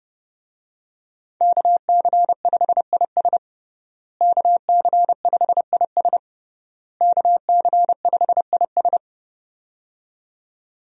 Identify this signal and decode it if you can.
Morse code, heard as a keyed tone, about 30 wpm, 700 Hz. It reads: KC5IS KC5IS KC5IS
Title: USA Call Signs - 5 Characters - Repeated 3x - 30wpm